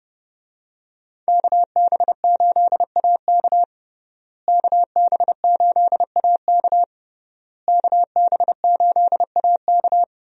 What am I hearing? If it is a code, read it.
Morse code, heard as a keyed tone, about 30 wpm, 700 Hz. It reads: KB8AK KB8AK KB8AK
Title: USA Call Signs - 5 Characters - Repeated 3x - 30wpm